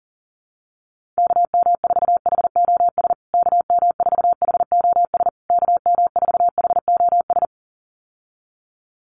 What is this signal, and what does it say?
Morse code, heard as a keyed tone, about 40 wpm, 700 Hz. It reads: KM4HOS KM4HOS KM4HOS
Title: USA Call Signs Repeated 3x 40wpm